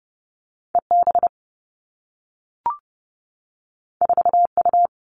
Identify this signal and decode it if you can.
Morse code, heard as a keyed tone, about 30 wpm, 700 Hz. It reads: E B 4 U
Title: Two Challenging Letter-Number Combinations 30wpm